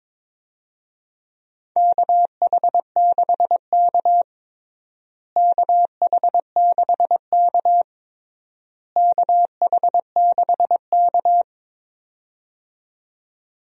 Morse code, heard as a keyed tone, about 22 wpm, 700 Hz. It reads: KH6K KH6K KH6K
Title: DXCC-100 Call Signs - 4 Characters - Repeated 3x - 22wpm